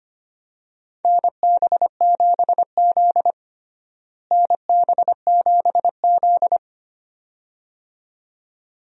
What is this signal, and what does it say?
Morse code, heard as a keyed tone, about 25 wpm, 700 Hz. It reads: NB7Z NB7Z
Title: USA Call Signs - 4 Characters - Repeated 2x - 25wpm